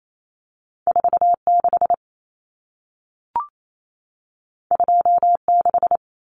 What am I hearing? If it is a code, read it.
Morse code, heard as a keyed tone, about 28 wpm, 700 Hz. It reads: 46 26
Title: Common Cut Numbers - 2-Digit 28wpm